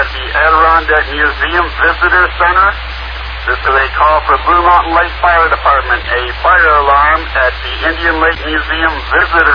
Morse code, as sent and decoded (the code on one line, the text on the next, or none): none